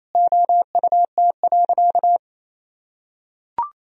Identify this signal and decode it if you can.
Morse code, heard as a keyed tone, about 28 wpm, 700 Hz. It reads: OUT.
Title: Sentences from Top 1000 Words - Review Part 1 - 28wpm